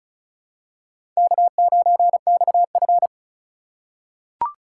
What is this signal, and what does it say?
Morse code, heard as a keyed tone, about 36 wpm, 700 Hz. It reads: K9XF
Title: USA Call Signs - 4 Characters - 2x Character Spacing - 35wpm